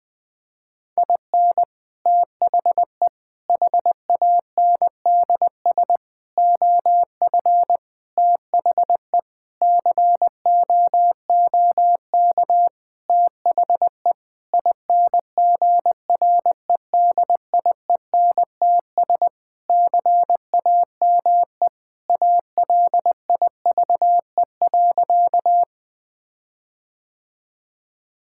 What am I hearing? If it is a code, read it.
Morse code, heard as a keyed tone, about 21 wpm, 700 Hz. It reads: IN THE HANDS OF THE COOK THE INGREDIENTS CAME ALIVE.
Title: Sentences with 5-Word Phrases - Spoken Beginning - 20wpm